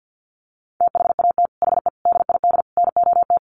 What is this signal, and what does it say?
Morse code, heard as a keyed tone, about 50 wpm, 700 Hz. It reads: THAT HE DID NOT
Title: Sets of 4 Words 50wpm